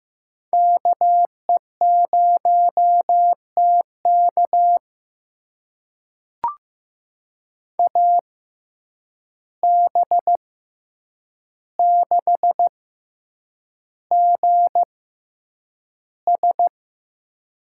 Morse code, heard as a keyed tone, about 15 wpm, spaced as for 8 wpm, 700 Hz. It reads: KE0TK AB6GS
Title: USA Call Signs - 5 Characters - 6x Character Spacing - 15wpm